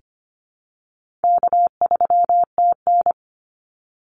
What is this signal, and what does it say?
Morse code, heard as a keyed tone, about 25 wpm, 700 Hz. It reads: K3TN
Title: Call Signs - 25wpm